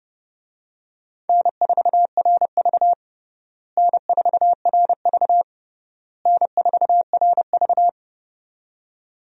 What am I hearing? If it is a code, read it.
Morse code, heard as a keyed tone, about 32 wpm, 700 Hz. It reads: N4RV N4RV N4RV
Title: USA Call Signs - 4 Characters - Repeated 3x - 30wpm